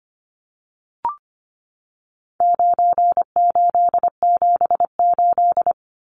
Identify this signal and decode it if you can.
Morse code, heard as a keyed tone, about 25 wpm, 700 Hz. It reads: 9878
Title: Numbers - 4-Digit 25wpm